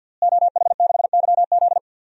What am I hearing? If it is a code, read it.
Morse code, heard as a keyed tone, about 47 wpm, 700 Hz. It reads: OH6YZ
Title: DXCC-100 Call Signs - 5 Characters - Repeated 2x - 50wpm